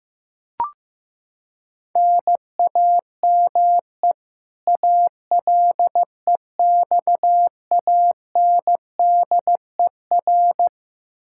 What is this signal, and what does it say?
Morse code, heard as a keyed tone, about 16 wpm, 700 Hz. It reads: NAME ALEXANDER
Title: Practice QSO - Exchange 1 - Beginning - 15wpm